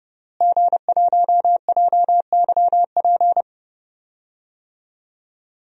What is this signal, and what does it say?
Morse code, heard as a keyed tone, about 31 wpm, 700 Hz. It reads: G1JYP
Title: DXCC-100 Call Signs - 5 Characters - Repeated 3x - 30wpm